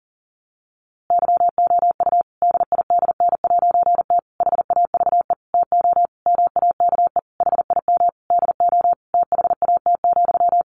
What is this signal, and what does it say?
Morse code, heard as a keyed tone, about 40 wpm, 700 Hz. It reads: YOU DIDN'T HAVE TO MAKE HIM DO THAT,
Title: A Sound Of Thunder 40wpm